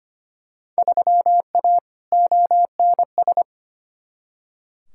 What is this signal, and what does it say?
Morse code, heard as a keyed tone, about 25 wpm, 700 Hz. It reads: 3A ONS
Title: Field Day Exchanges - 25wpm